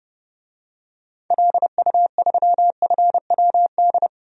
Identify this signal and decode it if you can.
Morse code, heard as a keyed tone, about 32 wpm, 700 Hz. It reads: LU3FWD
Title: DXCC-100 Call Signs - 6 Characters - 30wpm